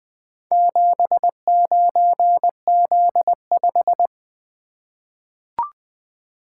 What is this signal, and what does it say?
Morse code, heard as a keyed tone, about 20 wpm, 700 Hz. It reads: 79Z5
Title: Four Letter-Number Combinations - 20wpm